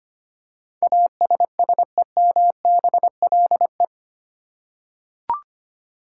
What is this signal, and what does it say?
Morse code, heard as a keyed tone, about 25 wpm, 700 Hz. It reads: ASSEMBLE
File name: 8 Letter Words - First 3 Letters Given - 25wpm